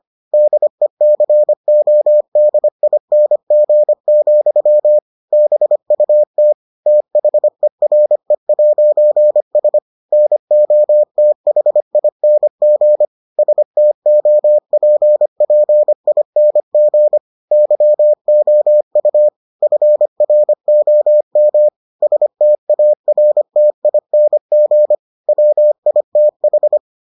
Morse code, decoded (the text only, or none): DECODING, BUT THERE'S NOTHING STOPPING YOU FROM STARTING WITH